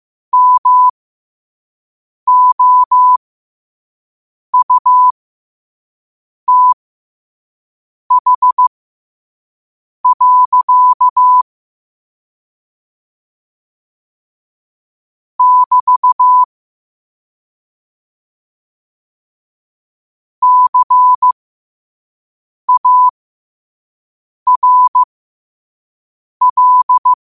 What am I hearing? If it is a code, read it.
Morse code, heard as a keyed tone, about 15 wpm, spaced as for 5 wpm, 1000 Hz. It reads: MOUTH. <BT> CARL
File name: qotdfiles / 5wpm